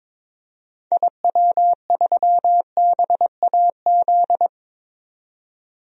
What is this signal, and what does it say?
Morse code, heard as a keyed tone, about 22 wpm, 700 Hz. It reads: IW3BAZ
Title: DXCC-100 Call Signs - 6 Characters - 22wpm